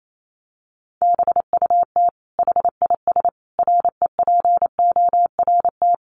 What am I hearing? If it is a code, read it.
Morse code, heard as a keyed tone, about 28 wpm, 700 Hz. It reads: BUT HIS REPORT
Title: Sentences from Top 1200 Words 28wpm